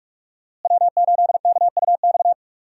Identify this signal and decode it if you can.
Morse code, heard as a keyed tone, about 46 wpm, 700 Hz. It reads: W8KUX
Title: USA Call Signs - 5 Characters - Repeated 2x - 45wpm